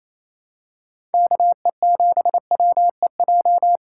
Morse code, heard as a keyed tone, about 28 wpm, 700 Hz. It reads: KE7WEJ
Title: USA Call Signs - 6 Characters - 2x Character Spacing - 28wpm